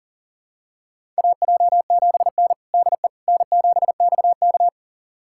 Morse code, heard as a keyed tone, about 40 wpm, 700 Hz. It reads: AJ7N DE N7XK
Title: USA Callsign DE Callsign - 4 Characters - 40wpm